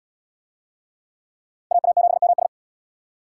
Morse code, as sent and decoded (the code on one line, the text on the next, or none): .- .- --... -- ...
AA7MS